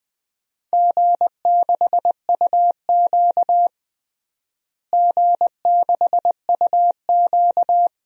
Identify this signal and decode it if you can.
Morse code, heard as a keyed tone, about 20 wpm, 700 Hz. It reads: G6UQ G6UQ
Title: DXCC-100 Call Signs - 4 Characters - Repeated 3x - 20wpm